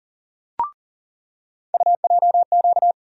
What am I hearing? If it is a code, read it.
Morse code, heard as a keyed tone, about 40 wpm, 700 Hz. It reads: UJQ
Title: USA Call Sign Postfixes - 3 Letters 40wpm